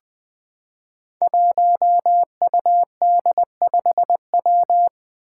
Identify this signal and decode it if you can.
Morse code, heard as a keyed tone, about 20 wpm, 700 Hz. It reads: 1UD5W
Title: Five Challenging Letter-Number Combinations 20wpm